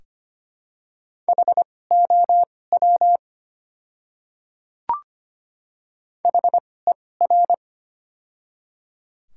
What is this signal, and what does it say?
Morse code, heard as a keyed tone, about 25 wpm, spaced as for 18 wpm, 690 Hz. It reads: HOW HER
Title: 2x Character Spacing - Top 100 Words - 25wpm